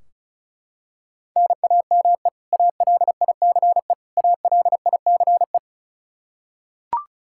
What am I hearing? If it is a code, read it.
Morse code, heard as a keyed tone, about 36 wpm, 700 Hz. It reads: NAME ALICE ALICE
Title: Practice QSO - Exchange 1 - Beginning - 35wpm